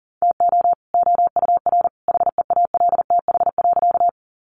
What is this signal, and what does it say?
Morse code, heard as a keyed tone, about 40 wpm, 700 Hz. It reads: TO OUR HEALTH.
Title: Sentences from Top 600 Words 40wpm